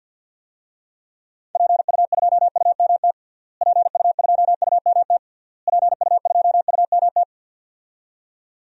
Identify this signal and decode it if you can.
Morse code, heard as a keyed tone, about 46 wpm, 700 Hz. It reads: PU2UMT PU2UMT PU2UMT
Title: DXCC-100 Call Signs - 6 Characters - Repeated 3x - 50wpm